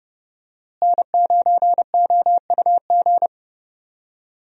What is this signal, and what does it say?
Morse code, heard as a keyed tone, about 30 wpm, 700 Hz. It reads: N9OUG
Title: USA Call Signs - 5 Characters - Repeated 2x - 30wpm